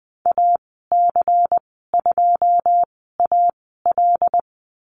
Morse code, heard as a keyed tone, about 20 wpm, spaced as for 15 wpm, 700 Hz. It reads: AC2AL
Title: 122 Lesson 20g - Number 3 - Call Signs 20wpm